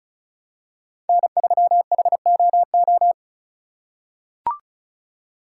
Morse code, heard as a keyed tone, about 34 wpm, 700 Hz. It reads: N3HOO
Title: USA Call Signs - 5 Characters - 2x Character Spacing - 35wpm